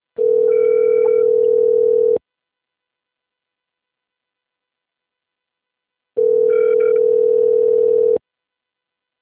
Ringback tone (North America), 440 + 480 Hz.